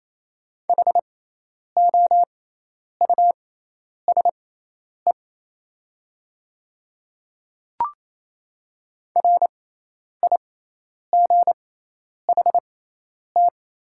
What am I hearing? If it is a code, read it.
Morse code, heard as a keyed tone, about 28 wpm, spaced as for 10 wpm, 690 Hz. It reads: HOUSE RIGHT
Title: 6x Character Spacing - Top 300 Words - 28wpm